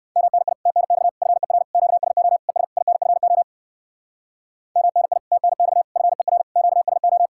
Morse code, heard as a keyed tone, about 56 wpm, 700 Hz. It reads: MNI TNX FER QSO ES INFO MNI TNX FER QSO
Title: Speed-Racing Common QSO Phrases Encore - 45wpm